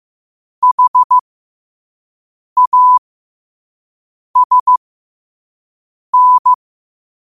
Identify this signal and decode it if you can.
Morse code, heard as a keyed tone, about 15 wpm, spaced as for 5 wpm, 1000 Hz. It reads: HASN
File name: 5wpm